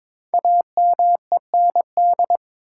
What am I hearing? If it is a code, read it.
Morse code, heard as a keyed tone, about 22 wpm, 700 Hz. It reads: AMEND
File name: 5 Letter Words - First 3 Letters Given - 22wpm